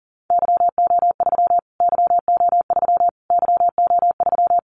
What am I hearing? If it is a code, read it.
Morse code, heard as a keyed tone, about 40 wpm, 700 Hz. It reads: YO3 YO3 YO3
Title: Callsign Prefixes for Top 25 DXCC - Repeated 3x - 40wpm